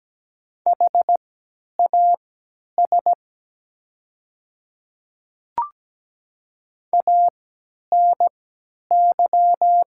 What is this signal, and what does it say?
Morse code, heard as a keyed tone, about 17 wpm, spaced as for 10 wpm, 700 Hz. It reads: HAS ANY
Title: 3x Character Spacing - Top 500 Words - 17wpm